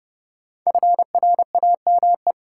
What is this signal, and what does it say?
Morse code, heard as a keyed tone, about 30 wpm, 700 Hz. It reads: FRAME
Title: Sentences from Top 1000 Words - Review Part 6 - New Word Drill - 30wpm